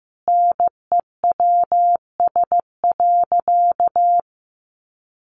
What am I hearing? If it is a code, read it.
Morse code, heard as a keyed tone, about 15 wpm, 700 Hz. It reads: NEWS.
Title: Sentences from Top 800 Words 15wpm